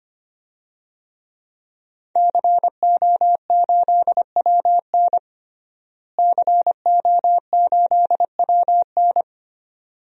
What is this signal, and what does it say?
Morse code, heard as a keyed tone, about 25 wpm, 700 Hz. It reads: CO8WN CO8WN
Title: DXCC-100 Call Signs - 5 Characters - Repeated 2x - 25wpm